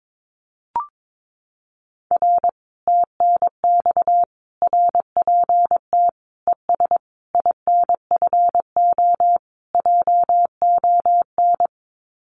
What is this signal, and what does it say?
Morse code, heard as a keyed tone, about 22 wpm, 700 Hz. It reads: R TNX RPT ES INFO JON
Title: Common QSO Phrases Encore 22wpm